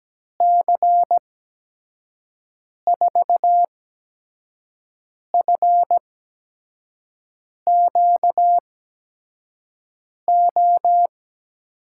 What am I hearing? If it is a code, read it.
Morse code, heard as a keyed tone, about 17 wpm, spaced as for 5 wpm, 700 Hz. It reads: C4FQO
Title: USA Call Signs - 6 Characters - 8x Character Spacing - 17wpm